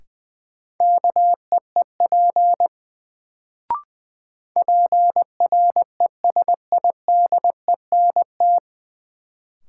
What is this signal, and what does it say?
Morse code, heard as a keyed tone, about 21 wpm, 700 Hz. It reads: KEEP PRESIDENT
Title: Sets of 2 Words - New Word Drill - 20wpm